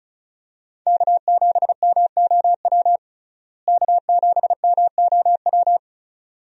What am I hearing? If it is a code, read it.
Morse code, heard as a keyed tone, about 34 wpm, 700 Hz. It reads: K7MOW K7MOW
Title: USA Call Signs - 5 Characters - Repeated 2x - 35wpm